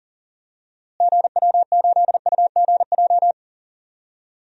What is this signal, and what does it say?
Morse code, heard as a keyed tone, about 40 wpm, 700 Hz. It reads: GW8UGJ
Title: DXCC-100 Call Signs - 6 Characters - Repeated 3x - 40wpm